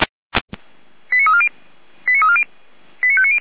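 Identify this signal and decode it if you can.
Five-tone selective call (ZVEI), heard as quick runs of five tones, tones of about 70 ms. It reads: 89340 89340 89590
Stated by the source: Five-tone ZVEI selective call 89330 received by SDR, channel 455.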